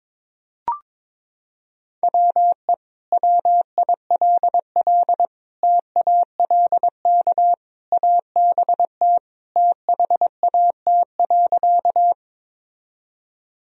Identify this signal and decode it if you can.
Morse code, heard as a keyed tone, about 22 wpm, 700 Hz. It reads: WE WILL TALK ABT THAT.
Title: Sentences from Top 400 Words and Abbreviations 22wpm